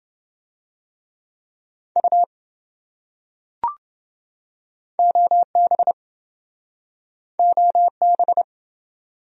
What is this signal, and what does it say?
Morse code, heard as a keyed tone, about 28 wpm, 700 Hz. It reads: U OB OB